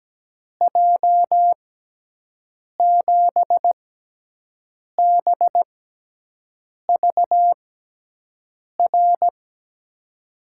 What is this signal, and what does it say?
Morse code, heard as a keyed tone, about 17 wpm, spaced as for 6 wpm, 700 Hz. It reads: J7BVR
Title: USA Call Signs - 6 Characters - 6x Character Spacing - 17wpm